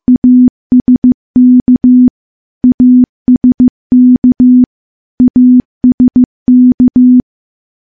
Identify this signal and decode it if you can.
Morse code, heard as a keyed tone, about 14 wpm, 250 Hz. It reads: ASK ASK ASK
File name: morse-training-15WPM